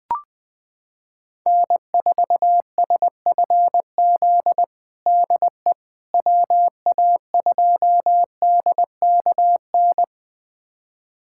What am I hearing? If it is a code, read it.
Morse code, heard as a keyed tone, about 21 wpm, 700 Hz. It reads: N4SFZ DE WA2DKN
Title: USA Callsign DE Callsign 20wpm